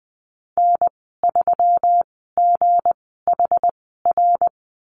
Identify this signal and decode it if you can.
Morse code, heard as a keyed tone, about 20 wpm, spaced as for 15 wpm, 700 Hz. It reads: N3GHR
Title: 186 Lesson 30g - Letter b - Call Signs 20wpm